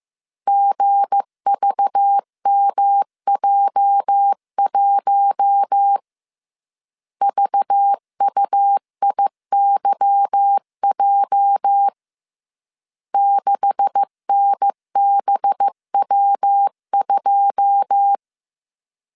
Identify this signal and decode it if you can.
Morse code, heard as a keyed tone, about 15 wpm, 800 Hz. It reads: GVMJ1 VUIYJ 6NBW2
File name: CW / DET / 1 9 1500C S C 14wpm 3245